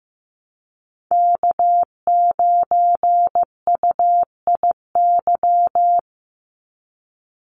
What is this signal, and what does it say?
Morse code, heard as a keyed tone, about 15 wpm, 700 Hz. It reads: K9UIY
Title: Call Signs Repeated 3x - 15wpm